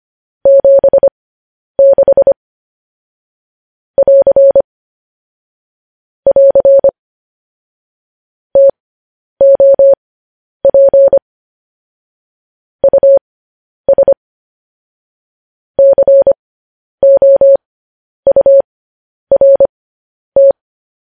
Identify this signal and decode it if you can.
Morse code, heard as a keyed tone, about 25 wpm, spaced as for 10 wpm, 550 Hz. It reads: 76 <AR> <AR> TOP US COURT